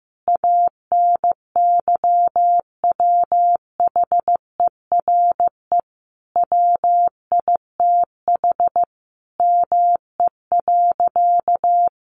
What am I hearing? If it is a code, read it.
Morse code, heard as a keyed tone, about 15 wpm, 700 Hz. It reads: ANYWHERE WITH ME.
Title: Sentences from Top 1400 Words 15wpm